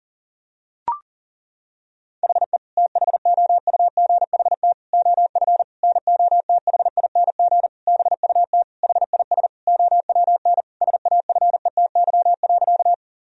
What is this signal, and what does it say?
Morse code, heard as a keyed tone, about 37 wpm, 700 Hz. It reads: HE THOUGHT OF NOTHING BUT HIS OWN SAFETY.
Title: Sentences from Top 1000 Words - Review Part 4 - 40wpm